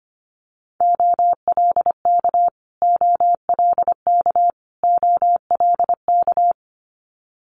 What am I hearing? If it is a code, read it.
Morse code, heard as a keyed tone, about 25 wpm, 700 Hz. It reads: OLK OLK OLK
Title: Multi-letter Phonemes Repeated 3x 25wpm